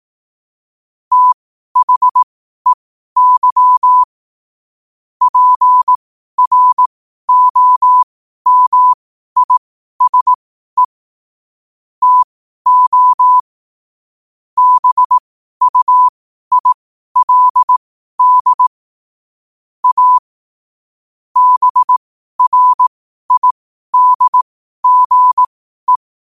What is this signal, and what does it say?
Morse code, heard as a keyed tone, about 18 wpm, spaced as for 12 wpm, 1000 Hz. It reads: THEY PROMISE TO BUILD A BRIDGE
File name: qotdfiles / 12wpm